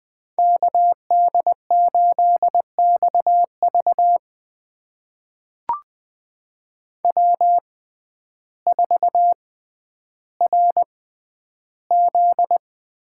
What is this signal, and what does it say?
Morse code, heard as a keyed tone, about 20 wpm, spaced as for 11 wpm, 700 Hz. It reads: KD8XV W4RZ
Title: USA Call Signs - 5 Characters - 6x Character Spacing - 20wpm